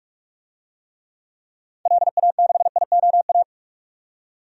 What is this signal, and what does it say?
Morse code, heard as a keyed tone, about 44 wpm, 700 Hz. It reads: LA6IOA
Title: DXCC-100 Call Signs - 6 Characters - 45wpm